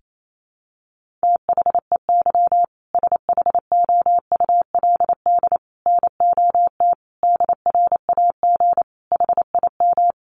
Morse code, heard as a keyed tone, about 28 wpm, 700 Hz. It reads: THEY SHOULD NOT DRAG HIM
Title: Sentences from Top 1500 Words 28wpm